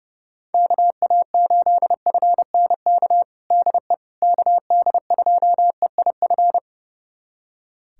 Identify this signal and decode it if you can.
Morse code, heard as a keyed tone, about 32 wpm, 690 Hz. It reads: KA8FNK DE KD2EIF
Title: USA Callsign DE Callsign - 6 Characters - 30wpm